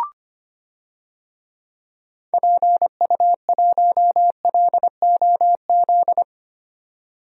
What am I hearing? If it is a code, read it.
Morse code, heard as a keyed tone, about 25 wpm, 700 Hz. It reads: PU1LOZ
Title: DXCC-100 Call Signs - 6 Characters - 25wpm